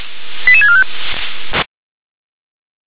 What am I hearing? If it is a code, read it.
Five-tone selective call (ZVEI), heard as quick runs of five tones, tones of about 70 ms. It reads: 88645